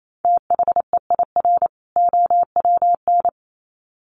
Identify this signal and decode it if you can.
Morse code, heard as a keyed tone, about 28 wpm, 700 Hz. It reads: THEIR OWN
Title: Sets of 2 Words 28wpm